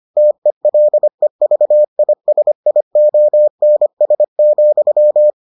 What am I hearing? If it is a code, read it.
Morse code, heard as a keyed tone, about 25 wpm, 600 Hz. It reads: TELEVISIONS,